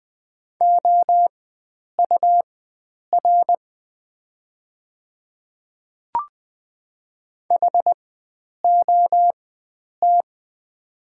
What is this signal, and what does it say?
Morse code, heard as a keyed tone, about 20 wpm, spaced as for 9 wpm, 700 Hz. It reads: OUR HOT
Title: 4x Character Spacing - Top 300 Words - 20wpm